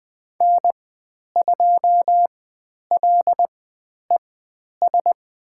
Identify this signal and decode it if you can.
Morse code, heard as a keyed tone, about 21 wpm, spaced as for 10 wpm, 700 Hz. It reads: N2LES
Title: 102 Lesson 17g - Letter u - Call Signs 20wpm